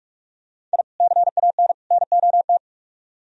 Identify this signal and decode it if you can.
Morse code, heard as a keyed tone, about 42 wpm, 700 Hz. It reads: I CAN NOT